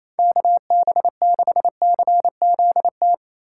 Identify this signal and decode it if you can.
Morse code, heard as a keyed tone, about 28 wpm, 700 Hz. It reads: KB6CZT
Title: USA Call Signs - 2x Character Spacing - 28wpm